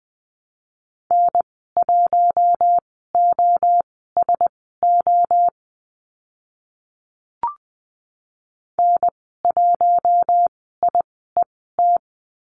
Morse code, heard as a keyed tone, about 20 wpm, spaced as for 15 wpm, 700 Hz. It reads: N1OSO N1IET
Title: 042 Lesson 8g - Number 1 - Call Signs 20wpm